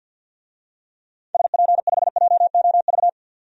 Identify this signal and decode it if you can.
Morse code, heard as a keyed tone, about 53 wpm, 700 Hz. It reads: SP5JOV
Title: DXCC-100 Call Signs - 6 Characters - 50wpm